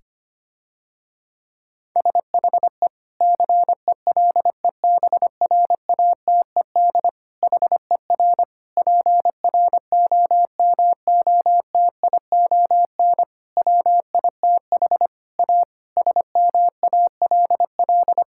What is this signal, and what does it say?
Morse code, heard as a keyed tone, about 25 wpm, 700 Hz. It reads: SHE CELEBRATED HER PROMOTION WITH A SMALL
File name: practice / 25wpm / Difficult Sentences with Hints - 25wpm